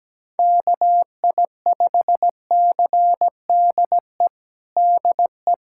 Morse code, heard as a keyed tone, about 17 wpm, 700 Hz. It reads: KI5CDE DE
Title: USA Callsign DE Callsign 17wpm